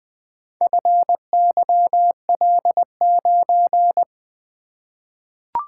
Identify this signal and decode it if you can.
Morse code, heard as a keyed tone, about 21 wpm, 700 Hz. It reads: FYL9
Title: Four Letter-Number Combinations - 20wpm